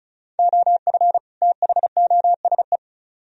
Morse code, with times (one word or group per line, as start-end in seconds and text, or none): 0.4-1.2 OF
1.4-2.8 THOSE